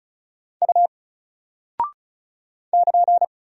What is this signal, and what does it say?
Morse code, heard as a keyed tone, about 36 wpm, 700 Hz. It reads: U <KN>
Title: Single Character (Letter:Number:Punctuation:Prosign) 35wpm